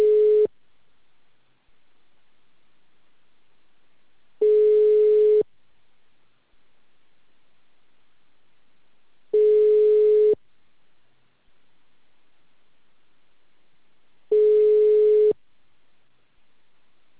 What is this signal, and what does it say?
Ringback tone (Europe, 425 Hz).
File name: out- -2044- - - .33504